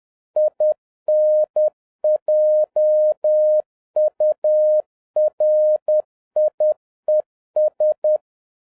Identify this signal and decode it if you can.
Morse code, heard as a keyed tone, about 10 wpm, 610 Hz. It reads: INJURIES
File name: 2011-07-mp3 / n56 BACK FROM A BIG ACCIDENT 10wpm